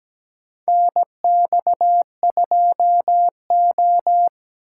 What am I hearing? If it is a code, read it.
Morse code, heard as a keyed tone, about 17 wpm, 700 Hz. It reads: NX2O
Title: USA Call Signs - 4 Characters - 6x Character Spacing - 17wpm